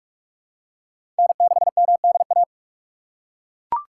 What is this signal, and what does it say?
Morse code, heard as a keyed tone, about 41 wpm, 700 Hz. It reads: N6MDA
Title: USA Call Signs - 5 Characters - 4x Character Spacing - 45wpm